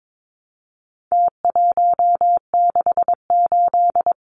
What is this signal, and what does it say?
Morse code, heard as a keyed tone, about 22 wpm, 700 Hz. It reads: T168
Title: Numbers - Contest Style - 4-Digit 22wpm